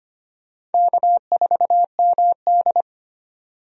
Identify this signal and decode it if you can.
Morse code, heard as a keyed tone, about 25 wpm, 700 Hz. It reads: K4MD
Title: USA Call Signs - 4 Characters - 25wpm